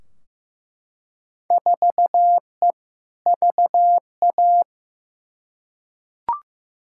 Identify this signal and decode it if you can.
Morse code, heard as a keyed tone, about 15 wpm, 700 Hz. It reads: 4E VA
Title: Field Day Exchanges Encore - 15wpm